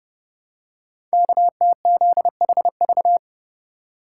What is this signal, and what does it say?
Morse code, heard as a keyed tone, about 30 wpm, 700 Hz. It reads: KTZHV
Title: Code Groups - 5 Letters - 30wpm